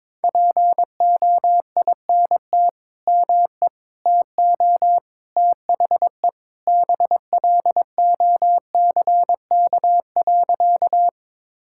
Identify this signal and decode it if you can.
Morse code, heard as a keyed tone, about 22 wpm, 700 Hz. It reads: POINT ME TO THE BLOCK.
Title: Sentences from Top 1600 Words 22wpm